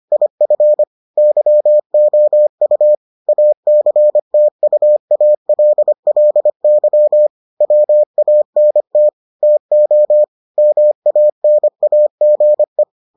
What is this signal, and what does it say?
Morse code, heard as a keyed tone, about 25 wpm, 600 Hz. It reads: IF YOU ACTUALLY WANT TO MANAGE